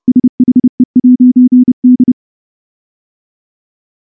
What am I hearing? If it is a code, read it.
Morse code, heard as a keyed tone, about 28 wpm, 250 Hz. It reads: SHE'D
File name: morse / morse-training-30WPM